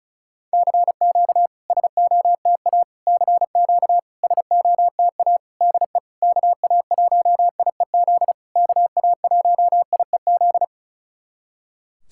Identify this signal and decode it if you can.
Morse code, heard as a keyed tone, about 36 wpm, 700 Hz. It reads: CQ SOTA CQ SOTA DE KA1IEZ KA1IEZ
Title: SOTA QSO Practice 35wpm